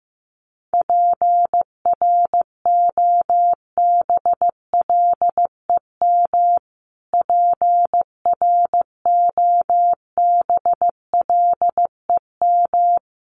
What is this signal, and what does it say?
Morse code, heard as a keyed tone, about 15 wpm, 700 Hz. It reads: PROBLEM PROBLEM
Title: Top 500 Words Repeated 15wpm